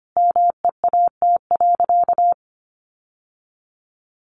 Morse code, heard as a keyed tone, about 25 wpm, 700 Hz. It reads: MEAT.
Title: Sentences from Top 1500 Words 25wpm